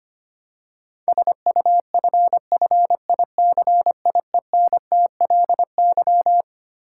Morse code, heard as a keyed tone, about 25 wpm, 700 Hz. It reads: SUFFICIENTLY
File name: Twelve Letter Words - 25wpm